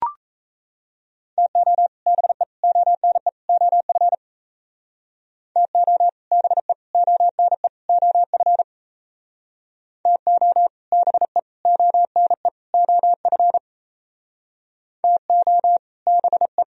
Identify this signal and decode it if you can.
Morse code, heard as a keyed tone, about 35 wpm, 700 Hz. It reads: TO BE ONE OF TO BE ONE OF TO BE ONE OF TO BE
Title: Speed-Racing - Sets of 4 Words 28wpm